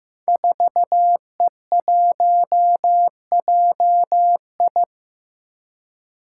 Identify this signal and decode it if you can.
Morse code, heard as a keyed tone, about 15 wpm, 700 Hz. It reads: 4E1JI
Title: Five Challenging Letter-Number Combinations 15wpm